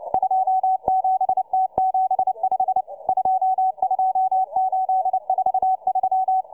Morse code, heard as a keyed tone, about 29 wpm, 760 Hz, with radio noise. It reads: 27T75 22843